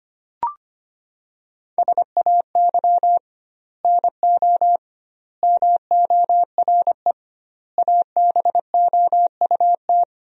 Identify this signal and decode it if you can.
Morse code, heard as a keyed tone, about 25 wpm, 700 Hz. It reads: SAY NO MORE ABOUT
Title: 2x Word-Spacing - Sentences from Top 200 Words - 25wpm